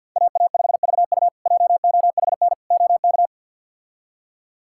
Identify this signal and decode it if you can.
Morse code, heard as a keyed tone, about 50 wpm, 700 Hz. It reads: AA5VU JOHN OK
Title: K1USN SST QSO Practice - 50wpm